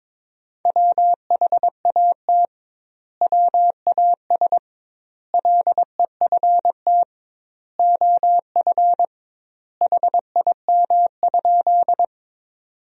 Morse code, heard as a keyed tone, about 22 wpm, 700 Hz. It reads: WHAT WAS LEFT OF HIM?
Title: 2x Word-Spacing - Sentences from Top 300 Words - 22wpm